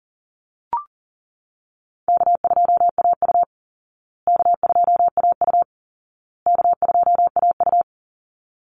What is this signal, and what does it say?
Morse code, heard as a keyed tone, about 37 wpm, 700 Hz. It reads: K2AU K2AU K2AU
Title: Easy Call Signs Repeated 3x - 40wpm